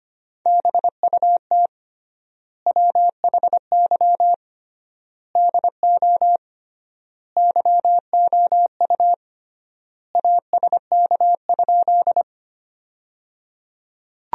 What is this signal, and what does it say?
Morse code, heard as a keyed tone, about 25 wpm, 700 Hz. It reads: BUT WHY DO YOU ASK?
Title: 3x Word-Spacing - Sentences from Top 200 Words - 25wpm